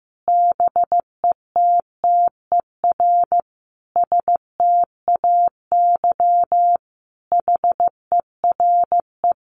Morse code, heard as a keyed tone, about 15 wpm, 700 Hz. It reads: BETTER STAY HERE
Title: Sentences from Top 500 Words 15wpm